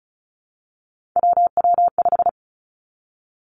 Morse code, heard as a keyed tone, about 36 wpm, 700 Hz. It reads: WW5
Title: Three Challenging Letter-Number Combinations 35wpm